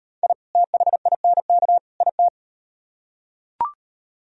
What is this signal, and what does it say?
Morse code, heard as a keyed tone, about 36 wpm, 700 Hz. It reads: I THINK IT